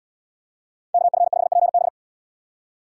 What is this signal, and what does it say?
Morse code, heard as a keyed tone, about 75 wpm, 700 Hz. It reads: KF5PB